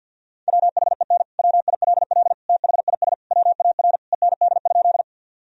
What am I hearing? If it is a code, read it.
Morse code, heard as a keyed tone, about 50 wpm, 700 Hz. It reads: WHEN WILL THIS WAR END?
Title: Sentences from Top 1000 Words - Review Part 2 - 50wpm